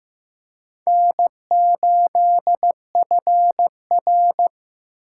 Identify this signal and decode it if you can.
Morse code, heard as a keyed tone, about 15 wpm, 700 Hz. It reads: N8FR
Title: USA Call Signs - 4 Characters - 6x Character Spacing - 15wpm